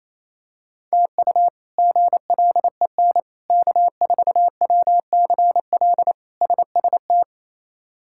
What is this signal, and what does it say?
Morse code, heard as a keyed tone, about 28 wpm, 700 Hz. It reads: TU GLEN K4WCL SST
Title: K1USN SST QSO Practice - 28wpm